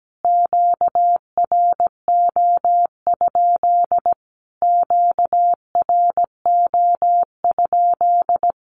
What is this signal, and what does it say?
Morse code, heard as a keyed tone, about 17 wpm, 700 Hz. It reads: QRO? QRO?
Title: Q-Codes Repeated 3x 17wpm